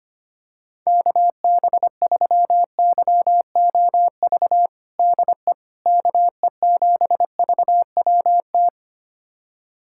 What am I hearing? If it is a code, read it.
Morse code, heard as a keyed tone, about 25 wpm, 700 Hz. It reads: KB3YOV DE KE7VWT
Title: USA Callsign DE Callsign - 6 Characters - 25wpm